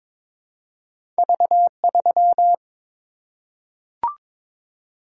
Morse code, heard as a keyed tone, about 22 wpm, 700 Hz. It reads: V3
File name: practice / 22wpm / Two Letter-Number Combinations - 22wpm